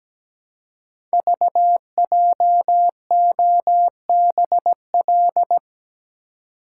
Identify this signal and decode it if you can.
Morse code, heard as a keyed tone, about 17 wpm, 700 Hz. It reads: VJOBL
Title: Code Groups - 5 Letters - 17wpm